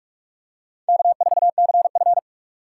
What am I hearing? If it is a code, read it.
Morse code, heard as a keyed tone, about 48 wpm, 700 Hz. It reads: K4CF
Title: USA Call Signs - 4 Characters - 2x Character Spacing - 45wpm